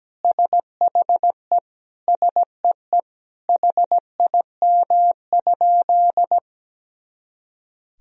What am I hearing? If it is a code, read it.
Morse code, heard as a keyed tone, about 17 wpm, 700 Hz. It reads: SHE SEE HIM?
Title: Sentences from Top 100 Words and Abbreviations 17wpm